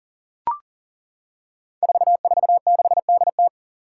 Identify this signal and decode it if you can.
Morse code, heard as a keyed tone, about 40 wpm, 700 Hz. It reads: 446DT
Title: Five Challenging Letter-Number Combinations 40wpm